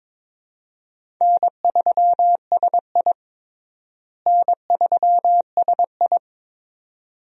Morse code, heard as a keyed tone, about 21 wpm, 700 Hz. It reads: N3SI N3SI
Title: USA Call Signs - 4 Characters - Repeated 2x - 22wpm